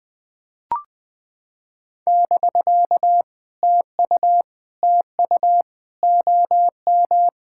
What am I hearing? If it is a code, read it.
Morse code, heard as a keyed tone, about 18 wpm, 700 Hz. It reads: <BK> TU TU OM
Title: POTA QSO Practice - 20wpm